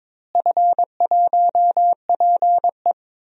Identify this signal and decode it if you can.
Morse code, heard as a keyed tone, about 22 wpm, 700 Hz. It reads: F1PE
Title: DXCC-100 Call Signs - 4 Characters - Repeated 3x - 22wpm